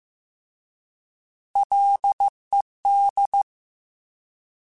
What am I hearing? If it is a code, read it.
Morse code, heard as a keyed tone, about 15 wpm, 800 Hz. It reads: LED
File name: Alphabetical / Alphabetical mp3s / Ls @13